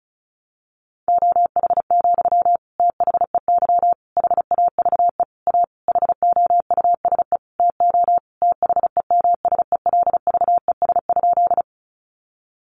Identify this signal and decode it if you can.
Morse code, heard as a keyed tone, about 34 wpm, 700 Hz. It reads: OH, THEY HAVE A HOUSE TO THEMSELVES?
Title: Sentences from Top 800 Words 35wpm